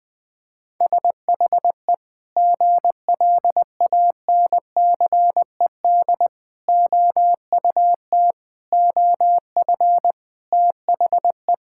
Morse code, heard as a keyed tone, about 20 wpm, 700 Hz. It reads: SHE GLANCED OUT OF THE
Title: Sentences with 5-Word Phrases - 20wpm